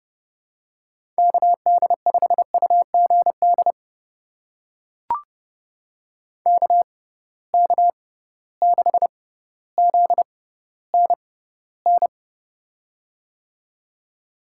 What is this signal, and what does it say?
Morse code, heard as a keyed tone, about 31 wpm, spaced as for 15 wpm, 700 Hz. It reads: KD5UGD KK6ZNN
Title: USA Call Signs - 6 Characters - 6x Character Spacing - 30wpm